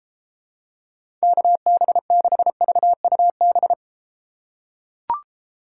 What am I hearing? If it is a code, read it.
Morse code, heard as a keyed tone, about 32 wpm, 700 Hz. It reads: KB6VUB